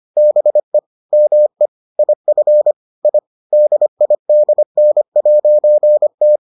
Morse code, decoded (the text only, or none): BE ME IF I DIDN'T